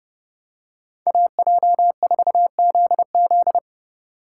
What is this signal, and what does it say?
Morse code, heard as a keyed tone, about 31 wpm, 700 Hz. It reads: AJ4ZZ